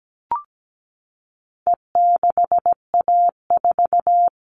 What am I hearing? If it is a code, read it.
Morse code, heard as a keyed tone, about 17 wpm, 700 Hz. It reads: E6A4
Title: All Cut Numbers - 4-Digit 17wpm